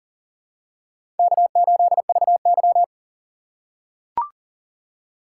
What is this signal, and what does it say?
Morse code, heard as a keyed tone, about 40 wpm, 700 Hz. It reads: K8VY
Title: USA Call Signs - 4 Characters - 3x Character Spacing - 40wpm